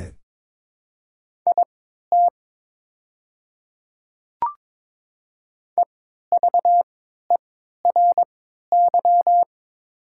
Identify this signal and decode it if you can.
Morse code, heard as a keyed tone, about 22 wpm, spaced as for 13 wpm, 700 Hz. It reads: IT EVERY
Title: 3x Character Spacing - Top 200 Words - 22wpm